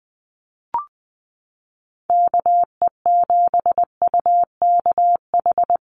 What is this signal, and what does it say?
Morse code, heard as a keyed tone, about 20 wpm, 700 Hz. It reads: KE7UKH
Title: USA Call Signs Repeated 3x 20wpm